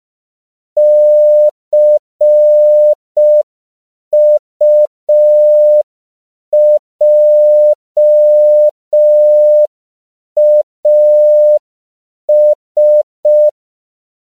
Morse code, heard as a keyed tone, about 5 wpm, 600 Hz. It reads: CUJAS
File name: CW / 5PPM